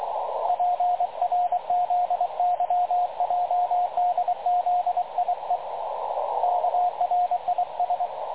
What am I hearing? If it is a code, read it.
Morse code, heard as a keyed tone, about 23 wpm, 720 Hz, with radio noise. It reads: PRZYJDZIE KRIS
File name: dlazaawansowanych / nagraniazpasma / websdr recording 3567.0kHz